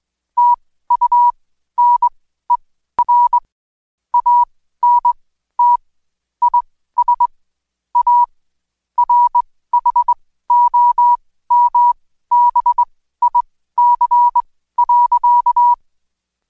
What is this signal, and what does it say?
Morse code, heard as a keyed tone, about 21 wpm, spaced as for 14 wpm, 970 Hz. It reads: TUNER ANT IS A RHOMBIC.
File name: Code Quick / AudioDownld2 / Seg123-Day28-DayDrill 28